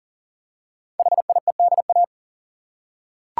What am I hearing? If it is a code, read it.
Morse code, heard as a keyed tone, about 42 wpm, 690 Hz. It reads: HIEDA
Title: Code Groups - 5 Letters - 40wpm